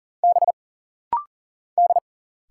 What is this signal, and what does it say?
Morse code, heard as a keyed tone, about 40 wpm, 700 Hz. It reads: B D
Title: Single Letter-Number Alt - Rapid-Fire 40wpm